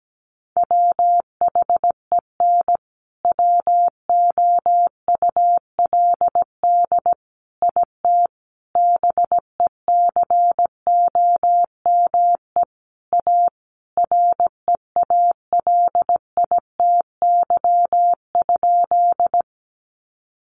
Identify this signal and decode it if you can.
Morse code, heard as a keyed tone, about 17 wpm, 700 Hz. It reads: WHEN WOULD IT BECOME A REALITY?
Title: Sentences from Top 1200 Words 17wpm